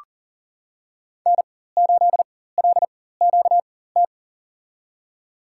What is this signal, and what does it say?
Morse code, heard as a keyed tone, about 40 wpm, spaced as for 19 wpm, 700 Hz. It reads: N8LQT
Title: USA Call Signs - 5 Characters - 4x Character Spacing - 40wpm